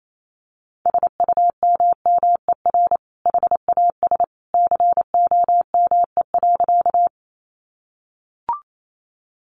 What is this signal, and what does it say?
Morse code, heard as a keyed tone, about 28 wpm, 700 Hz. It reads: SUMMER HAS COME.
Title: Sentences from Top 1100 Words 28wpm